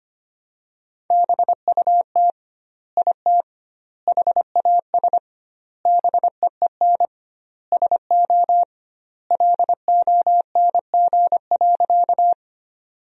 Morse code, heard as a keyed tone, about 25 wpm, 700 Hz. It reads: BUT IT HAS BEEN SO LONG.
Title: 2x Word-Spacing - Sentences from Top 100 Words - 25wpm